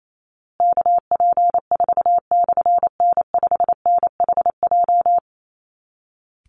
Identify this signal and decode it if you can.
Morse code, heard as a keyed tone, about 28 wpm, 700 Hz. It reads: KP4/N5NHJ
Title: Call Signs Repeated 3x - 28wpm